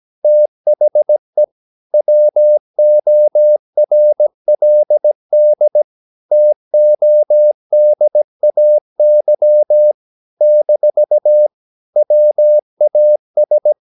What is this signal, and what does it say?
Morse code, heard as a keyed tone, about 17 wpm, 600 Hz. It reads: THE WORLD TODAY - WAS
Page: Romanen Never af Ken Follet på engelsk.